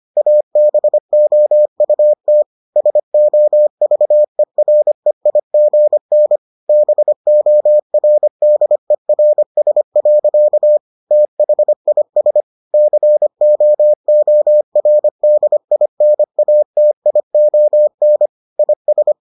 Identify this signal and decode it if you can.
Morse code, heard as a keyed tone, about 25 wpm, 600 Hz. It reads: ABOUT SOVEREIGN BORDERS. THIS COORDINATION IS